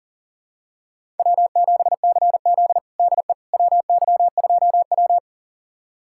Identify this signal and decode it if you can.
Morse code, heard as a keyed tone, about 40 wpm, 700 Hz. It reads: W7CZ DE WY2W
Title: USA Callsign DE Callsign - 4 Characters - 40wpm